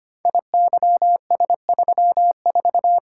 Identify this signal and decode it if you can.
Morse code, heard as a keyed tone, about 25 wpm, 690 Hz. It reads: IYS34
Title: Code Groups - 5 Letter:Numbers - 25wpm